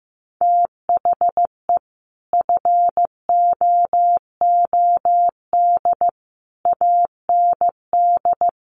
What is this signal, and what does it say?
Morse code, heard as a keyed tone, about 15 wpm, 700 Hz. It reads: THE FOOD AND
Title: Sets of 5 Words 15wpm